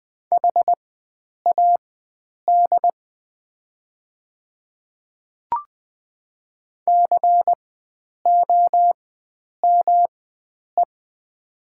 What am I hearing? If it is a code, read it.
Morse code, heard as a keyed tone, about 20 wpm, spaced as for 9 wpm, 700 Hz. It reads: HAD COME
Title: 4x Character Spacing - Top 100 Words - 20wpm